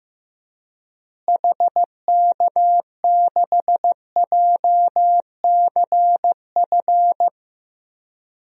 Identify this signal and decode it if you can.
Morse code, heard as a keyed tone, about 15 wpm, 700 Hz. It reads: HK6JCF
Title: DXCC-100 Call Signs - 6 Characters - Repeated 2x - 15wpm